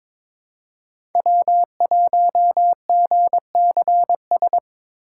Morse code, heard as a keyed tone, about 21 wpm, 700 Hz. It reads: W1GCS
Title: K1USN SST QSO Practice - 22wpm